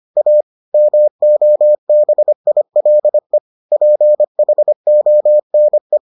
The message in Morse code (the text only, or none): A MOBILE PHONE